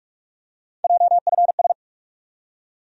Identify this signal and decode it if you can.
Morse code, heard as a keyed tone, about 46 wpm, 700 Hz. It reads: JFS
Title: USA Call Sign Postfixes - 3 Letters 45wpm